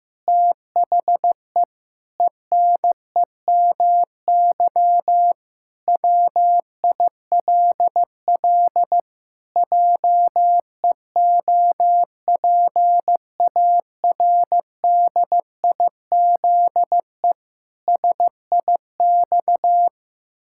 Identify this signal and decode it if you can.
Morse code, heard as a keyed tone, about 16 wpm, 700 Hz. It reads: THE ENEMY WILL JEOPARDIZE SIX
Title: Pangram Challenge v2 - 15wpm